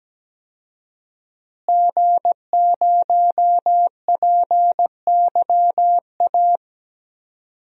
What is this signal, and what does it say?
Morse code, heard as a keyed tone, about 17 wpm, 700 Hz. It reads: G0PYA
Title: DXCC-100 Call Signs - 5 Characters - Repeated 3x - 17wpm